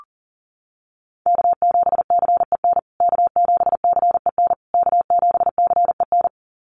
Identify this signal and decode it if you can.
Morse code, heard as a keyed tone, about 40 wpm, 700 Hz. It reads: K7CEN K7CEN K7CEN
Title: USA Call Signs Repeated 3x 40wpm